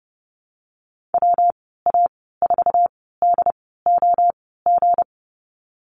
Morse code, heard as a keyed tone, about 30 wpm, spaced as for 17 wpm, 700 Hz. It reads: WA4DOG
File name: Lesson 37g - Letter x - Call Signs 30wpm